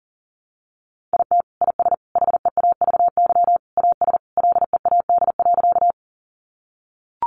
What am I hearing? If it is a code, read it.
Morse code, heard as a keyed tone, about 40 wpm, 700 Hz. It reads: IT IS HEAVY AS LEAD.
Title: Sentences from Top 1100 Words 40wpm